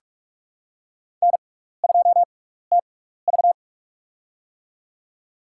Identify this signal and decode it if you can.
Morse code, heard as a keyed tone, about 45 wpm, spaced as for 15 wpm, 700 Hz. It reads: N2TV
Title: USA Call Signs - 4 Characters - 6x Character Spacing - 45wpm